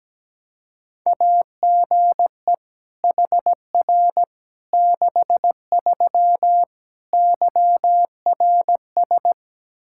Morse code, heard as a keyed tone, about 17 wpm, 700 Hz. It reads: AGE HR 63 YRS
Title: K1USN SST QSO Practice 17wpm